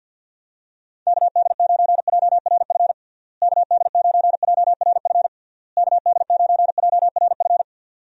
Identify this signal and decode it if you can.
Morse code, heard as a keyed tone, about 51 wpm, 700 Hz. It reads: KD9JRF KD9JRF KD9JRF
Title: USA Call Signs - 6 Characters - Repeated 3x - 50wpm